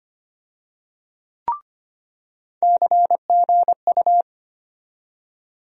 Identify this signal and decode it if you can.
Morse code, heard as a keyed tone, about 25 wpm, 700 Hz. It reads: CGU
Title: USA Call Sign Postfixes - 3 Letters 25wpm